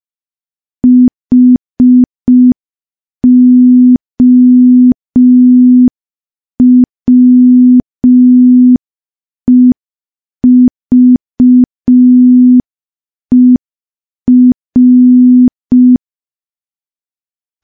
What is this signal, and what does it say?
Morse code, heard as a keyed tone, about 5 wpm, 250 Hz. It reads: HOWEVER
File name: morse / morse-training-05WPM